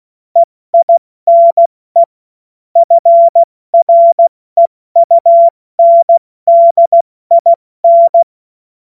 Morse code, heard as a keyed tone, about 16 wpm, 680 Hz. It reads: EINE FREUNDIN
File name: Morsekurs / Witze / Witz 05. April 2022 70 F80